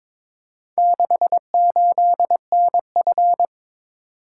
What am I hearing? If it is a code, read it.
Morse code, heard as a keyed tone, about 22 wpm, 700 Hz. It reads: 68NF
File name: practice / 22wpm / Four Letter-Number Combinations - 22wpm